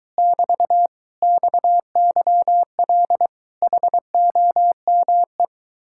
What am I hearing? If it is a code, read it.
Morse code, heard as a keyed tone, about 23 wpm, 700 Hz. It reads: <BT> XYL HOME